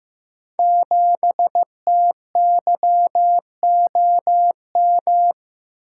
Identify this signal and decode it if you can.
Morse code, heard as a keyed tone, about 15 wpm, 700 Hz. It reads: 7TYOM
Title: Code Groups - 5 Letter:Numbers - 15wpm